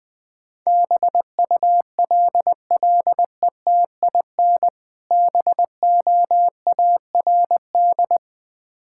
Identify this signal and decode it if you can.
Morse code, heard as a keyed tone, about 20 wpm, 700 Hz. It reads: BULLETIN BOARD
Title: Easy Words - Speed-Racing with Hints - 20wpm